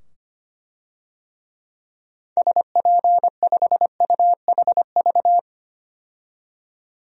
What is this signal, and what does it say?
Morse code, heard as a keyed tone, about 25 wpm, 700 Hz. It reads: SP5UHV
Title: DXCC-100 Call Signs - 6 Characters - 25wpm